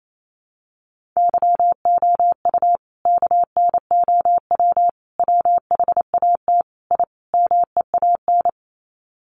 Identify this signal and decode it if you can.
Morse code, heard as a keyed tone, about 28 wpm, 700 Hz. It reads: YOU KNOW WHAT I MEAN
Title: Sets of 5 Words 28wpm